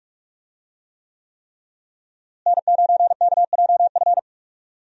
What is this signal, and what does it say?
Morse code, heard as a keyed tone, about 41 wpm, 700 Hz. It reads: N9KJF
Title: USA Call Signs - 5 Characters - Repeated 3x - 45wpm